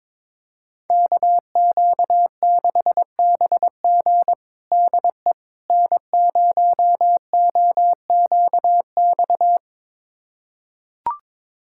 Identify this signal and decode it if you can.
Morse code, heard as a keyed tone, about 22 wpm, 700 Hz. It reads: KQ6BG DE N0OQX
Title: USA Callsign DE Callsign - 5 Characters - 22wpm